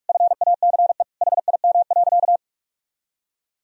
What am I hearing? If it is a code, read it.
Morse code, heard as a keyed tone, about 41 wpm, 700 Hz. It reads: FACE HIM.
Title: Sentences from Top 1000 Words - Review Part 2 - 45wpm